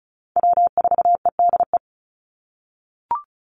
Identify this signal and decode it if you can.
Morse code, heard as a keyed tone, about 34 wpm, 700 Hz. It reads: W4EDE
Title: Call Signs - 35wpm